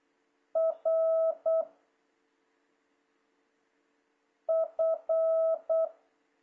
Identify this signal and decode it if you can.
Morse code, heard as a keyed tone, about 8 wpm, 660 Hz. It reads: R F
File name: jh8cbh / 2026cw